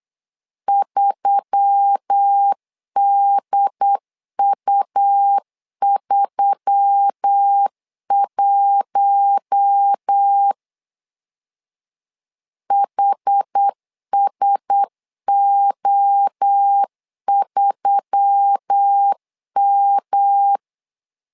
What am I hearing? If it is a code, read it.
Morse code, heard as a keyed tone, about 8 wpm, 800 Hz. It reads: 3DU31 HSO3M
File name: LEZIONI / 4 Q1S3 8wpm 3037